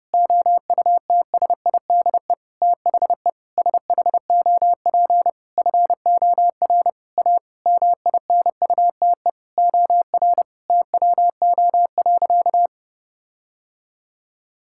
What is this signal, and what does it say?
Morse code, heard as a keyed tone, about 30 wpm, 700 Hz. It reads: OUTSIDE THE SHOP FOR A MINUTE OR TWO.
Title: Sentences from Top 1600 Words 30wpm